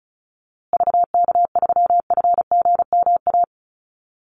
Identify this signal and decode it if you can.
Morse code, heard as a keyed tone, about 34 wpm, 700 Hz. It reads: VK3FGMA
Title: USA Call Signs Repeated 2x 35wpm